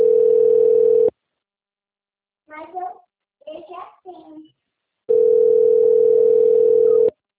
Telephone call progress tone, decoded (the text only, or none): ringback tone